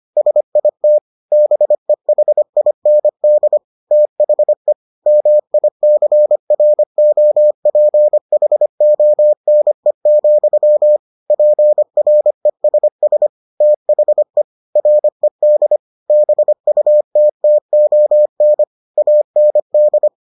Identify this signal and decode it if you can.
Morse code, heard as a keyed tone, about 25 wpm, 600 Hz. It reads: SIT BEHIND THE MICROPHONE, PRESS THE RED BUTTON AND